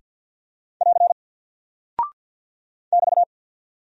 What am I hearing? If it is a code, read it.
Morse code, heard as a keyed tone, about 50 wpm, 700 Hz. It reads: <AR> <BT>
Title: Single Character (Letter:Number:Punctuation:Prosign) 50wpm